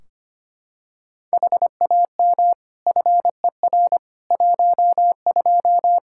Morse code, heard as a keyed tone, about 25 wpm, 700 Hz. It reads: HAM FER 12
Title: Practice QSO - Exchange 3 - More Info - 25wpm